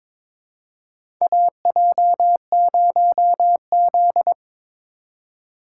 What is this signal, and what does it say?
Morse code, heard as a keyed tone, about 21 wpm, 700 Hz. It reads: AJ0Z
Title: USA Call Signs - 4 Characters - 4x Character Spacing - 22wpm